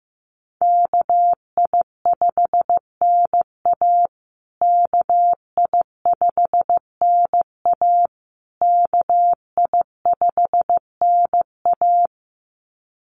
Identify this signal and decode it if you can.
Morse code, heard as a keyed tone, about 15 wpm, 700 Hz. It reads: KI5NA KI5NA KI5NA
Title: USA Call Signs Repeated 3x 15wpm